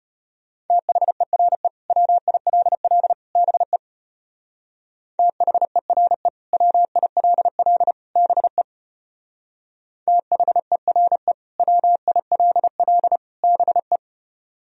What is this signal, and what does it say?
Morse code, heard as a keyed tone, about 33 wpm, 700 Hz. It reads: THERE WILL BE THERE WILL BE THERE WILL BE